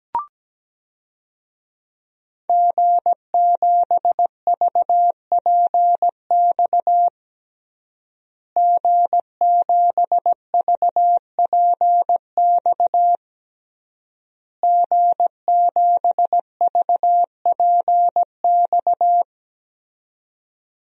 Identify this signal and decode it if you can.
Morse code, heard as a keyed tone, about 17 wpm, 700 Hz. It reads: G7VPX G7VPX G7VPX
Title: DXCC-100 Call Signs - 5 Characters - Repeated 3x - 17wpm